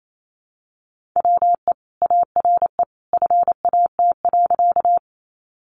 Morse code, heard as a keyed tone, about 28 wpm, 700 Hz. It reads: WE ARE FAT.
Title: Sentences from Top 600 Words 28wpm